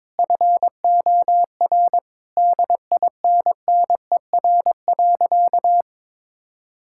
Morse code, heard as a keyed tone, about 22 wpm, 700 Hz. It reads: FOR DINNER.
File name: practice / 22wpm / Easy Sentences with Hints - 22wpm